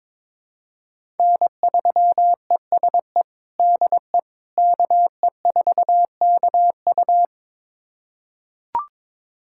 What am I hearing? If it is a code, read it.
Morse code, heard as a keyed tone, about 22 wpm, 700 Hz. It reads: N3ESE DE KE4KU
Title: USA Callsign DE Callsign - 5 Characters - 22wpm